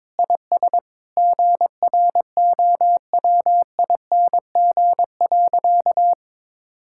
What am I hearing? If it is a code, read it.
Morse code, heard as a keyed tone, about 22 wpm, 700 Hz. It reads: IS GROWING.
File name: practice / 22wpm / Sentences with 8-Word Phrases - 22wpm